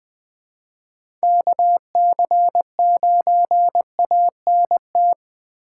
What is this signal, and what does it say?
Morse code, heard as a keyed tone, about 20 wpm, 700 Hz. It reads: KC9ANT
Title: USA Call Signs - 6 Characters - 6x Character Spacing - 20wpm